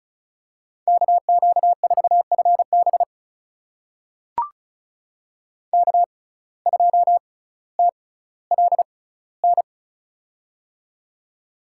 Morse code, heard as a keyed tone, about 36 wpm, spaced as for 18 wpm, 700 Hz. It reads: KQ4FB K2TLN
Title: USA Call Signs - 5 Characters - 6x Character Spacing - 35wpm